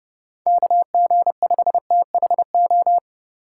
Morse code, heard as a keyed tone, about 35 wpm, 700 Hz. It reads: KG5THO
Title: Speed Racing - USA 2x3 Call Signs 30wpm